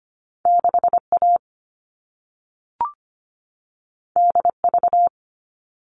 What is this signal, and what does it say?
Morse code, heard as a keyed tone, about 25 wpm, 700 Hz. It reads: 6A DV
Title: All Cut Numbers - 2-Digit 25wpm